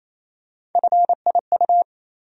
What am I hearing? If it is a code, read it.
Morse code, heard as a keyed tone, about 27 wpm, 700 Hz. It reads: FIU
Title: USA Call Sign Postfixes - 3 Letters 28wpm